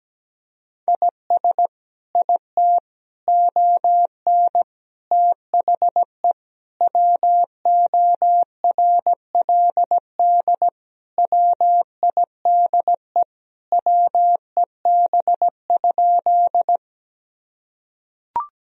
Morse code, heard as a keyed tone, about 17 wpm, 700 Hz. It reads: IS IT ON THE WORLD WIDE WEB?
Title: Sentences from Top 1000 Words - Review Part 4 - 17wpm